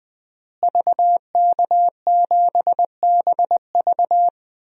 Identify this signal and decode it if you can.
Morse code, heard as a keyed tone, about 21 wpm, 700 Hz. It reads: VK7BV
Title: DXCC-100 Call Signs - 5 Characters - Repeated 2x - 20wpm